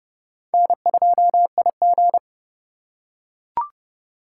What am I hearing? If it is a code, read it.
Morse code, heard as a keyed tone, about 27 wpm, 700 Hz. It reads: N2IG